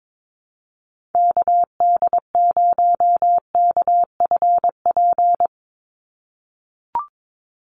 Morse code, heard as a keyed tone, about 22 wpm, 700 Hz. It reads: KD0KFP
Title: USA Call Signs 22wpm